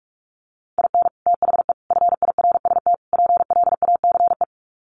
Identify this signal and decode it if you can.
Morse code, heard as a keyed tone, about 44 wpm, 700 Hz. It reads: IN THE FIRST PLACE
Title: Sets of 4 Words 45wpm